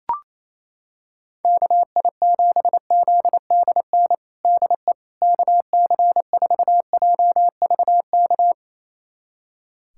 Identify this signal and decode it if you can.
Morse code, heard as a keyed tone, about 28 wpm, 700 Hz. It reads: KI7ZDN DE KC4JVK
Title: USA Callsign DE Callsign - 6 Characters - 28wpm